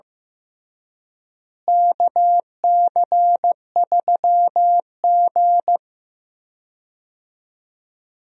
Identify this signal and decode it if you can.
Morse code, heard as a keyed tone, about 15 wpm, 700 Hz. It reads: KC3G
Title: USA Call Signs - 4 Characters - Repeated 3x - 15wpm